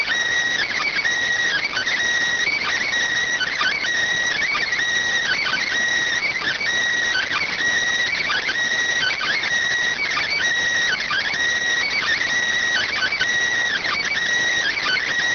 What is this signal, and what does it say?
Emergency Alert System (SAME) tones (header bursts), heard as harsh buzzing bursts. It is heard from 0.9 to 14.7 s.